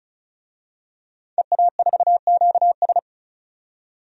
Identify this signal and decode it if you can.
Morse code, heard as a keyed tone, about 36 wpm, 700 Hz. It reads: EA4QS
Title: DXCC-100 Call Signs - 5 Characters - 35wpm